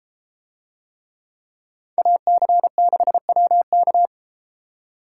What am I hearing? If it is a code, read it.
Morse code, heard as a keyed tone, about 33 wpm, 700 Hz. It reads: AC6WK